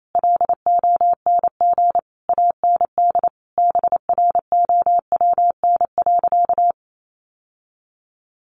Morse code, heard as a keyed tone, about 28 wpm, 700 Hz. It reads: LONG AND BROWN.
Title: Sentences from Top 1200 Words 28wpm